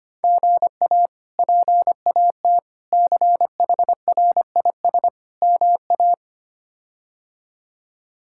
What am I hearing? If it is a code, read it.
Morse code, heard as a keyed tone, about 25 wpm, 700 Hz. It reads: GA PAT CHRIS MA
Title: K1USN SST QSO Practice - 25wpm